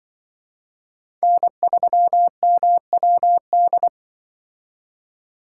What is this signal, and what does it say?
Morse code, heard as a keyed tone, about 24 wpm, 700 Hz. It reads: N3MWD
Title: Speed Racing - USA 1x3 Call Signs 20wpm